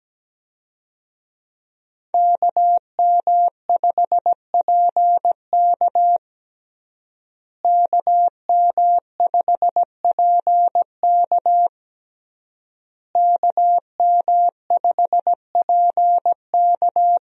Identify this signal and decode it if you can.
Morse code, heard as a keyed tone, about 17 wpm, 700 Hz. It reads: KM5PK KM5PK KM5PK
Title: USA Call Signs - 5 Characters - Repeated 3x - 17wpm